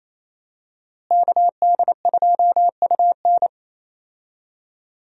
Morse code, heard as a keyed tone, about 28 wpm, 700 Hz. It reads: KD2UN
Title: USA Call Signs - 5 Characters - Repeated 3x - 28wpm